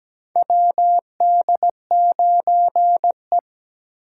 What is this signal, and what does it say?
Morse code, heard as a keyed tone, about 17 wpm, 700 Hz. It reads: WD9E
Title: USA Call Signs - 8x Character Spacing - 17wpm